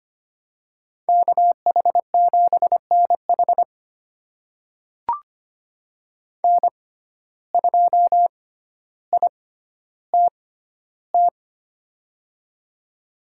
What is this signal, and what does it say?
Morse code, heard as a keyed tone, about 25 wpm, spaced as for 13 wpm, 700 Hz. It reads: KH7NH N2ITT
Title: USA Call Signs - 5 Characters - 6x Character Spacing - 25wpm